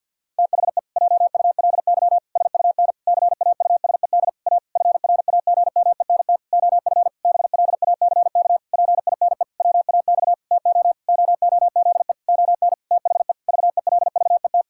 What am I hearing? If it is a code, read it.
Morse code, heard as a keyed tone, about 50 wpm, 700 Hz. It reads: THE JULY SUN CAUSED A FRAGMENT OF BLACK PINE WAX TO OOZE ON THE VELVET
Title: Pangram Challenge v2 - 50wpm